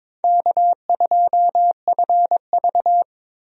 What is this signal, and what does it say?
Morse code, heard as a keyed tone, about 22 wpm, 700 Hz. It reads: K2FV
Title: USA Call Signs - 4 Characters - Repeated 3x - 22wpm